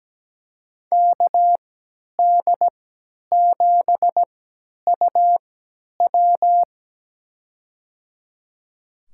Morse code, heard as a keyed tone, about 17 wpm, spaced as for 10 wpm, 700 Hz. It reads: KD7UW
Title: USA Call Signs - 5 Characters - 3x Character Spacing - 17wpm